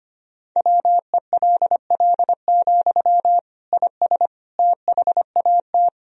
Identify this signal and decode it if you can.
Morse code, heard as a keyed tone, about 25 wpm, 700 Hz. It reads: WELL, IS THAT
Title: Sentences from Top 1000 Words - Review Part 1 - 25wpm